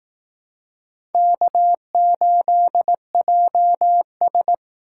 Morse code, heard as a keyed tone, about 18 wpm, 700 Hz. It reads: K8JS
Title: Speed Racing - USA 1x2 Call Signs 15wpm